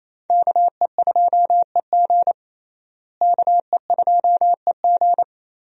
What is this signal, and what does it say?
Morse code, heard as a keyed tone, about 28 wpm, 700 Hz. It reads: KE2EG KE2EG
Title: USA Call Signs - 5 Characters - Repeated 2x - 28wpm